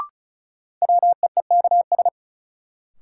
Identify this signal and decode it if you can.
Morse code, heard as a keyed tone, about 34 wpm, 700 Hz. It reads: WEEKS
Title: Sets of 5 Words - New Word Drill - 35wpm